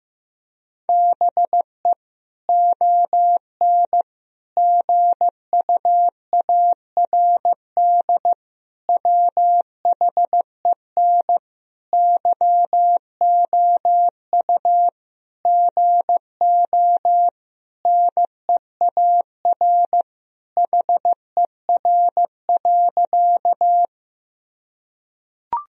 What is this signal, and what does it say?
Morse code, heard as a keyed tone, about 16 wpm, 700 Hz. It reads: BE ON GUARD WHEN YOU GO NEAR HER.
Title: Sentences from Top 1000 Words - Review Part 4 - 15wpm